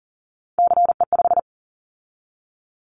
Morse code, heard as a keyed tone, about 42 wpm, 700 Hz. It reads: CE5
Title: Callsign Prefixes for Top 100 DXCC - 40wpm